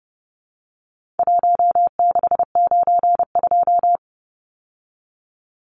Morse code, heard as a keyed tone, about 30 wpm, 700 Hz. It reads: 1692
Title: Numbers - 4-Digit 30wpm